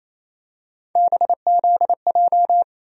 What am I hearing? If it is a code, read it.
Morse code, heard as a keyed tone, about 28 wpm, 700 Hz. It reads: BZJ
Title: USA Call Sign Postfixes - 3 Letters 28wpm